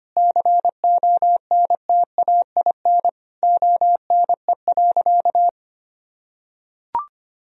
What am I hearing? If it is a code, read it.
Morse code, heard as a keyed tone, about 25 wpm, 700 Hz. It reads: CONTAIN ONE.
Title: Sentences from Top 1000 Words - Review Part 2 - 25wpm